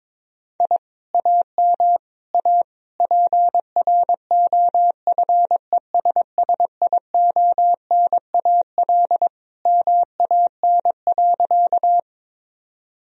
Morse code, heard as a keyed tone, about 22 wpm, 700 Hz. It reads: I AM A PROFESSIONAL MAN.
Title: Sentences from Top 1000 Words - Review Part 4 - 22wpm